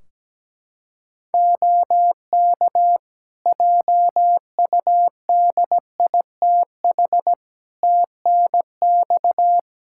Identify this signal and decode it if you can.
Morse code, heard as a keyed tone, about 17 wpm, 700 Hz. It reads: OK JUDITH TNX
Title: Practice QSO - Exchange 4 - Ending - 17wpm